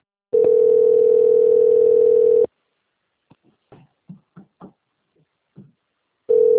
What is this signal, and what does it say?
Ringback tone (North America), 440 + 480 Hz.